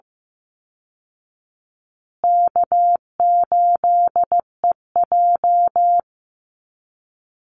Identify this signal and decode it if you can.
Morse code, heard as a keyed tone, about 14 wpm, 700 Hz. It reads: K8EJ
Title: Call Signs Repeated 2x - 15wpm